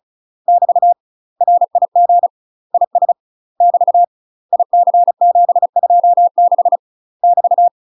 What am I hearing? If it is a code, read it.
Morse code, heard as a keyed tone, about 36 wpm, 700 Hz. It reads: <BT> RIG IS <BT> IC726 <BT>
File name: UK MASTER QSOs TRAINING/35WPM